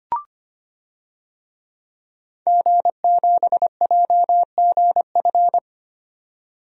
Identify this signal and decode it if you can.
Morse code, heard as a keyed tone, about 25 wpm, 700 Hz. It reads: G7JGF
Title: DXCC-100 Call Signs - 5 Characters - 25wpm